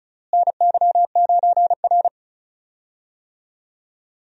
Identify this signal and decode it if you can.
Morse code, heard as a keyed tone, about 36 wpm, 700 Hz. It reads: NY9R
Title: USA Call Signs - 4 Characters - Repeated 3x - 35wpm